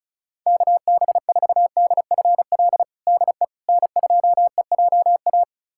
Morse code, heard as a keyed tone, about 34 wpm, 700 Hz. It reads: KB4DFL DE N2EJA
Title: USA Callsign DE Callsign 35wpm